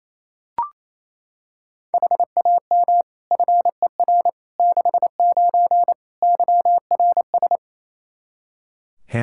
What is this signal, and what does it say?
Morse code, heard as a keyed tone, about 28 wpm, 700 Hz. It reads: HAM FER 69 YRS
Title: K1USN SST QSO Practice 28wpm